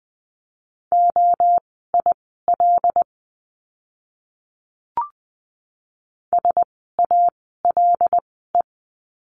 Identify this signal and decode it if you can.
Morse code, heard as a keyed tone, about 20 wpm, spaced as for 15 wpm, 700 Hz. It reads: OIL SALE
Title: 075 Lesson 13f - Letter L - Words 20wpm